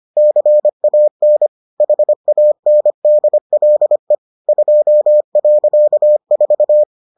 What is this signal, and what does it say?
Morse code, heard as a keyed tone, about 25 wpm, 600 Hz. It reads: CAN HANDLE 2.4